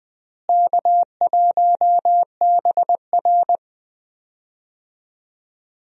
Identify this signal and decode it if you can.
Morse code, heard as a keyed tone, about 21 wpm, 700 Hz. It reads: K1BR
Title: USA Call Signs - 4 Characters - Repeated 3x - 20wpm